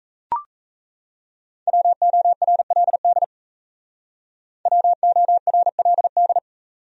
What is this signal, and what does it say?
Morse code, heard as a keyed tone, about 36 wpm, 700 Hz. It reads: WORLD WORLD